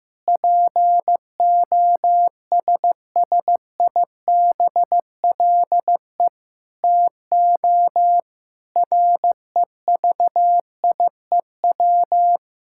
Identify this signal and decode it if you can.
Morse code, heard as a keyed tone, about 15 wpm, 700 Hz. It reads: POSSIBLE TO REVIEW
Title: Sentences from Top 1000 Words - Review Part 6 - 15wpm